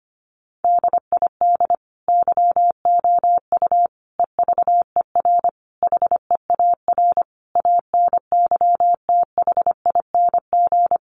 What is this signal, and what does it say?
Morse code, heard as a keyed tone, about 25 wpm, 700 Hz. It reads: DID YOU EVER HEAR ANYTHING
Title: Sentences from Top 1400 Words 25wpm